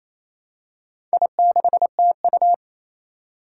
Morse code, heard as a keyed tone, about 28 wpm, 700 Hz. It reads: I6TU
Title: Four Challenging Letter-Number Combinations 28wpm